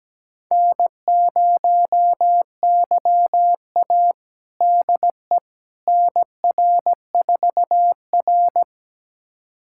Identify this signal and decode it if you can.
Morse code, heard as a keyed tone, about 17 wpm, 700 Hz. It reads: N0YA DE NR4R
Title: USA Callsign DE Callsign - 4 Characters - 17wpm